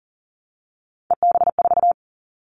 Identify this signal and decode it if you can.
Morse code, heard as a keyed tone, about 40 wpm, 700 Hz. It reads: EB4
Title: Callsign Prefixes for Top 25 DXCC - Repeated 3x - 40wpm